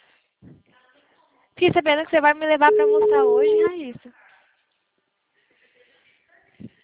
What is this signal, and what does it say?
Ringback tone (Europe, 425 Hz), with speech.
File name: out- -2006- - - .5579